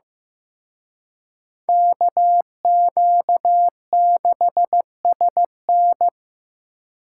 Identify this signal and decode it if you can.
Morse code, heard as a keyed tone, about 18 wpm, 700 Hz. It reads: KQ6SN
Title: Speed Racing - USA 2x2 Call Signs 15wpm